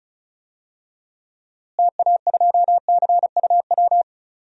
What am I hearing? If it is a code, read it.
Morse code, heard as a keyed tone, about 36 wpm, 700 Hz. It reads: TA2CUW
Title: DXCC-100 Call Signs - 6 Characters - Repeated 2x - 35wpm